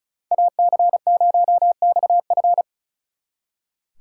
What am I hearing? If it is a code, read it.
Morse code, heard as a keyed tone, about 45 wpm, 700 Hz. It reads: AC0XF